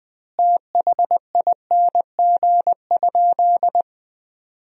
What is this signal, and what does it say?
Morse code, heard as a keyed tone, about 19 wpm, 700 Hz. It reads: THING?
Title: 2x Word-Spacing - Sentences from Top 200 Words - 20wpm